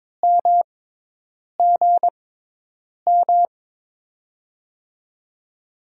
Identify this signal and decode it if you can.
Morse code, heard as a keyed tone, about 22 wpm, spaced as for 8 wpm, 700 Hz. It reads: MGM
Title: USA Call Signs - 6 Characters - 6x Character Spacing - 22wpm